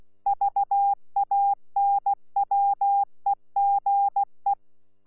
Morse code, heard as a keyed tone, about 16 wpm, 800 Hz. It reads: VANWEGE
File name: Z PI4VRZ / PI4VRZ 14-jun-2025